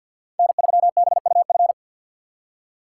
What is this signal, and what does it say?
Morse code, heard as a keyed tone, about 58 wpm, 700 Hz. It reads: N3BUF
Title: Speed Racing - USA Call Signs 50wpm